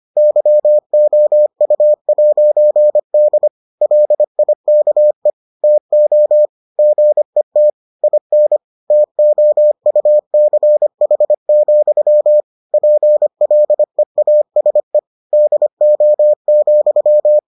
Morse code, heard as a keyed tone, about 25 wpm, 600 Hz. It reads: YOU'D LIKE TO GET IN TOUCH, PLEASE DO,